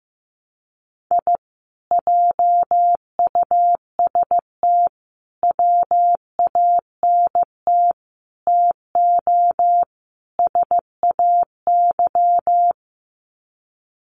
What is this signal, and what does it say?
Morse code, heard as a keyed tone, about 15 wpm, 700 Hz. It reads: I JUST WANT TO SAY
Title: Sets of 5 Words Encore 15wpm